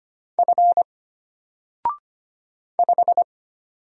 Morse code, heard as a keyed tone, about 25 wpm, 700 Hz. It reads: F 5
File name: Single-Letter-Number-25wpm